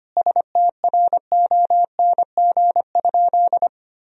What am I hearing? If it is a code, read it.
Morse code, heard as a keyed tone, about 25 wpm, 700 Hz. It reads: STRONG?
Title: Sentences from Top 1000 Words - Review Part 3 - 25wpm